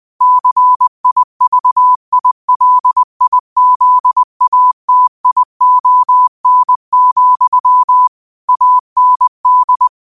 Morse code, heard as a keyed tone, about 20 wpm, 1000 Hz. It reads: CIVILIZATION, AND